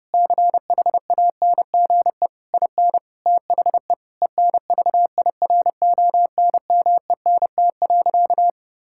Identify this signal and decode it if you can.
Morse code, heard as a keyed tone, about 30 wpm, 700 Hz. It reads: CHANGE IN THE ENVIRONMENT.
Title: Sentences from Top 1000 Words - Review Part 6 - 30wpm